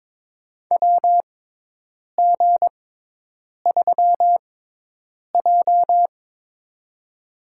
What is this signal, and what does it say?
Morse code, heard as a keyed tone, about 22 wpm, spaced as for 8 wpm, 700 Hz. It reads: WG3J
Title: USA Call Signs - 4 Characters - 6x Character Spacing - 22wpm